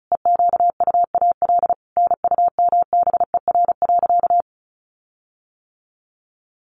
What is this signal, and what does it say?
Morse code, heard as a keyed tone, about 34 wpm, 700 Hz. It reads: EQUAL NUMBER.
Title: Sentences from Top 1400 Words 35wpm